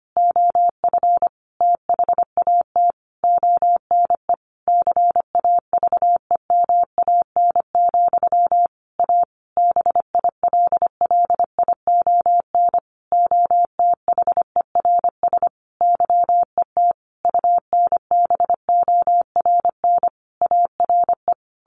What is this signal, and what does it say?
Morse code, heard as a keyed tone, about 25 wpm, 700 Hz. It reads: OF THAT ONE CAVEMAN, A BILLION OTHERS YET UNBORN ARE
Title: A Sound Of Thunder 25wpm